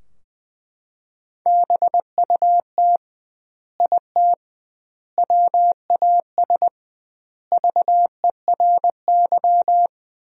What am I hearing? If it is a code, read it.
Morse code, heard as a keyed tone, about 20 wpm, 700 Hz. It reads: BUT IT WAS VERY
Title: 2x Word-Spacing - Sentences from Top 300 Words - 20wpm